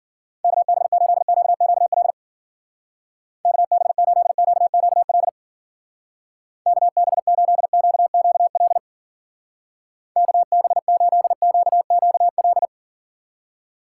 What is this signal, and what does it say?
Morse code, heard as a keyed tone, about 51 wpm, 700 Hz. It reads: KB8QQL KB8QQL KB8QQL KB8QQL